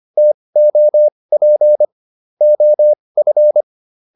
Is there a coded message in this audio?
Yes: Morse code from 0.2 to 3.6 s, heard as a keyed tone, about 25 wpm, spaced as for 20 wpm, 600 Hz. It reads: TOP OF